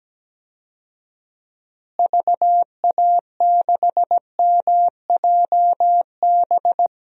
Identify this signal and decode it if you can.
Morse code, heard as a keyed tone, about 17 wpm, 700 Hz. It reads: VA6MJB
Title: DXCC-100 Call Signs - 6 Characters - Repeated 2x - 17wpm